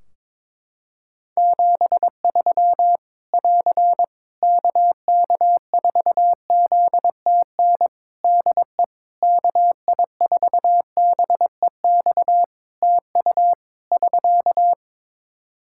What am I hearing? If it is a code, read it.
Morse code, heard as a keyed tone, about 22 wpm, 700 Hz. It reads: 73 <AR> KK4ZTN DE KI4BEX TU <SK>
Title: K1USN SST QSO Practice 22wpm